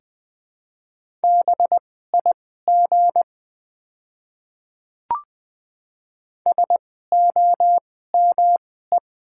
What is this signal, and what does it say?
Morse code, heard as a keyed tone, about 20 wpm, spaced as for 14 wpm, 700 Hz. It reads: BIG SOME
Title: 2x Character Spacing - Top 200 Words - 20wpm